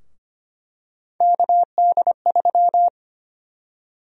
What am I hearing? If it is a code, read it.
Morse code, heard as a keyed tone, about 25 wpm, 700 Hz. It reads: KD3
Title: USA Call Sign Prefixes - 25wpm